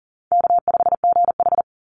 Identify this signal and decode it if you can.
Morse code, heard as a keyed tone, about 41 wpm, 700 Hz. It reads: K5GH
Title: Easy Call Signs - 40wpm